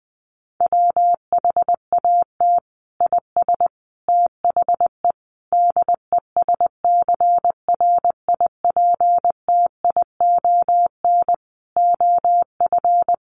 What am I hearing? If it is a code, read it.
Morse code, heard as a keyed tone, about 20 wpm, 700 Hz. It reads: WHAT IS THE DESCRIPTION OF
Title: Sentences from Top 1300 Words 20wpm